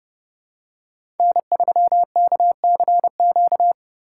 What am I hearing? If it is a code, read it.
Morse code, heard as a keyed tone, about 30 wpm, 700 Hz. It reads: N3KCQ
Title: USA Call Signs - 5 Characters - 3x Character Spacing - 30wpm